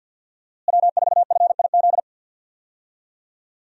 Morse code, heard as a keyed tone, about 47 wpm, 700 Hz. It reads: W4FIZ
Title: USA Call Signs - 5 Characters - Repeated 2x - 50wpm